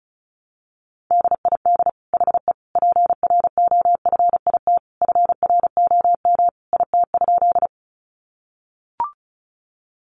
Morse code, heard as a keyed tone, about 34 wpm, 700 Hz. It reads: DID HE PROFIT FROM IT?
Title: Sentences from Top 800 Words 35wpm